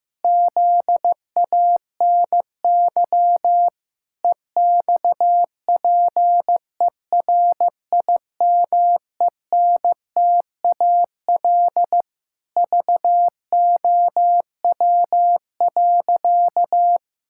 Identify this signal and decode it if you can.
Morse code, heard as a keyed tone, about 15 wpm, 700 Hz. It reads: ZANY EXPERIMENTAL VOW.
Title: Pangram Challenge v2 - 15wpm